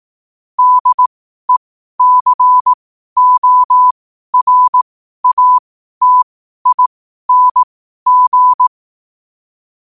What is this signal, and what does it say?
Morse code, heard as a keyed tone, about 18 wpm, spaced as for 12 wpm, 1000 Hz. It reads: DECORATING
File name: qotdfiles / 12wpm